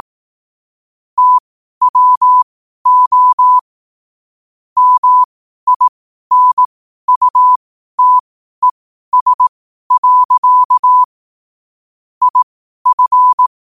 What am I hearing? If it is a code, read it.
Morse code, heard as a keyed tone, about 18 wpm, spaced as for 12 wpm, 1000 Hz. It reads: TWO MINUTES. IF
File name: qotdfiles / 12wpm